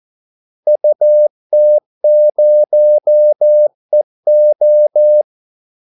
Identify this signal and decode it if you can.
Morse code, heard as a keyed tone, about 14 wpm, 600 Hz. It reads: UT0EO
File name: Callsigns-14wpm